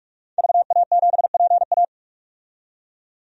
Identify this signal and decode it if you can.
Morse code, heard as a keyed tone, about 46 wpm, 700 Hz. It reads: VA7PA
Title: DXCC-100 Call Signs - 5 Characters - 45wpm